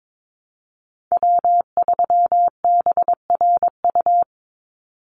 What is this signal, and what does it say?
Morse code, heard as a keyed tone, about 22 wpm, 700 Hz. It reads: W3BRU
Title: USA Call Signs 22wpm